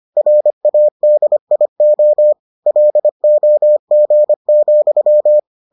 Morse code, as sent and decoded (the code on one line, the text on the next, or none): .-. .- -.. .. --- / .-.. --- --. --..--
RADIO LOG,